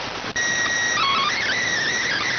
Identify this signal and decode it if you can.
SSTV (slow-scan television), PD 290.